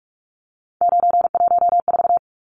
Morse code, heard as a keyed tone, about 44 wpm, 700 Hz. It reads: 914
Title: Numbers - 3-Digit 45wpm